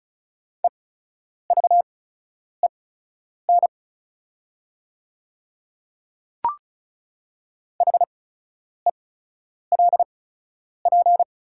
Morse code, heard as a keyed tone, about 36 wpm, spaced as for 10 wpm, 690 Hz. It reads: EVEN HELP
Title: 8x Character Spacing - Top 200 Words - 35wpm